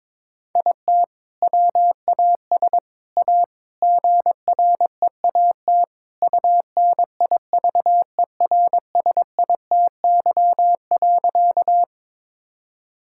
Morse code, heard as a keyed tone, about 21 wpm, 700 Hz. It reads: IT WAS A GREAT UNIVERSITY.
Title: Sentences from Top 1000 Words - Review Part 5 - 22wpm